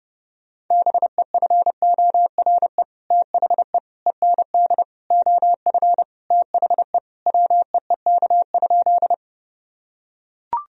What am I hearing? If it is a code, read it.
Morse code, heard as a keyed tone, about 30 wpm, 700 Hz. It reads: BEFORE THE END OF THE WEEK?
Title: Sentences from Top 1000 Words - Review Part 2 - 30wpm